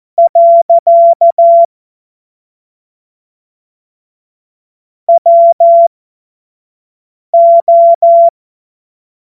Morse code, heard as a keyed tone, about 14 wpm, spaced as for 5 wpm, 680 Hz. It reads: . WO
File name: Morsekurs / Witze / Witz 04. Juli 2023 25 F70